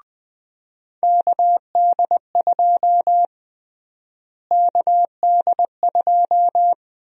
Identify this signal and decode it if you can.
Morse code, heard as a keyed tone, about 21 wpm, 700 Hz. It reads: KD2 KD2
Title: USA Call Sign Prefixes - Repeated 2x - 20wpm